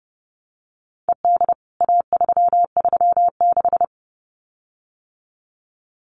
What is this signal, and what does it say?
Morse code, heard as a keyed tone, about 30 wpm, 700 Hz. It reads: ED A336
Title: CWT Exchange 30wpm